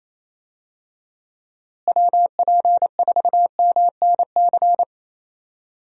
Morse code, heard as a keyed tone, about 27 wpm, 700 Hz. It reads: WP4MNC
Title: USA Call Signs - 6 Characters - Repeated 2x - 28wpm